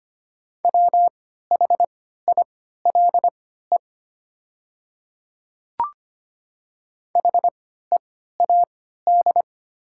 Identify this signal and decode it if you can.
Morse code, heard as a keyed tone, about 25 wpm, spaced as for 14 wpm, 700 Hz. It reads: WHILE HEAD
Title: 3x Character Spacing - Top 300 Words - 25wpm